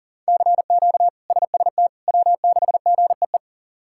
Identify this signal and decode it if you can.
Morse code, heard as a keyed tone, about 40 wpm, 700 Hz. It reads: CQ SST W6GEE
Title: K1USN SST QSO Practice - 40wpm